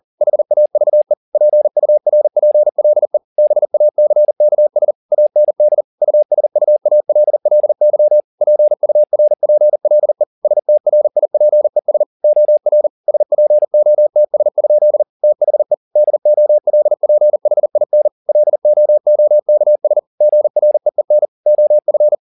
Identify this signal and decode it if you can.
Morse code, heard as a keyed tone, about 40 wpm, 600 Hz. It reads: HAVE PURPLE BACKS AND USUALLY PURPLE STRIPES OR SPOTS? THE DOLPHIN LOOKS GREEN OF
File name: the old man and the sea 2 40wpm